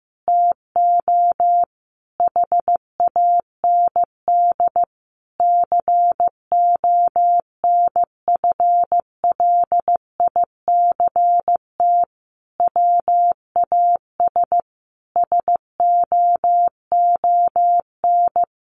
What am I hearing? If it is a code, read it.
Morse code, heard as a keyed tone, about 15 wpm, 700 Hz. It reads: TO HAND CONFLICT WAS SOON
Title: Sentences from Top 1300 Words 15wpm